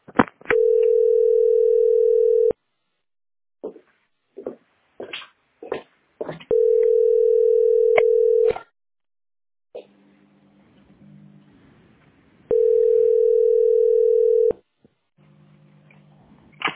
Ringback tone (North America), 440 + 480 Hz.